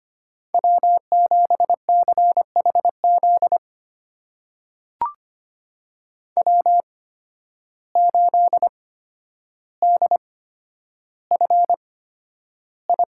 Morse code, heard as a keyed tone, about 25 wpm, spaced as for 11 wpm, 700 Hz. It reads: W7CHZ W8DFI
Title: USA Call Signs - 5 Characters - 8x Character Spacing - 25wpm